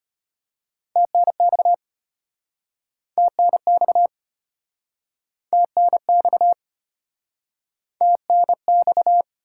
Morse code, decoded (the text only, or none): TNX TNX TNX TNX